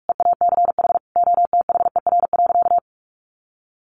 Morse code, heard as a keyed tone, about 44 wpm, 700 Hz. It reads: EACH OTHER.
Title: Sentences from Top 1300 Words 45wpm